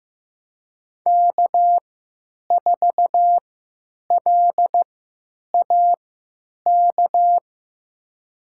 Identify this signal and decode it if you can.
Morse code, heard as a keyed tone, about 15 wpm, spaced as for 9 wpm, 700 Hz. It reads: K4LAK
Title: USA Call Signs - 5 Characters - 3x Character Spacing - 15wpm